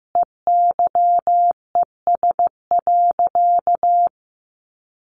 Morse code, heard as a keyed tone, about 15 wpm, 700 Hz. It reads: EYES.
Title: Easy Phrases 15wpm